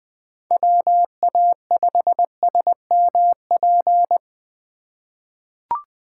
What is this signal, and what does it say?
Morse code, heard as a keyed tone, about 20 wpm, 700 Hz. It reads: WA5SMP
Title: USA Call Signs - 6 Characters - 2x Character Spacing - 20wpm